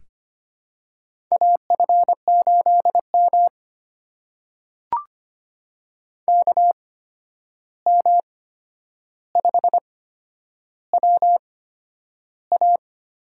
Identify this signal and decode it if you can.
Morse code, heard as a keyed tone, about 25 wpm, spaced as for 10 wpm, 700 Hz. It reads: AF8M KM5WA
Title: USA Call Signs - 8x Character Spacing - 25wpm